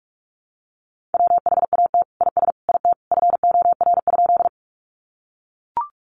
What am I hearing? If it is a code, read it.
Morse code, heard as a keyed tone, about 44 wpm, 700 Hz. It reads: WHAT IS IT FOR?
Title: Sentences from Top 100 Words 45wpm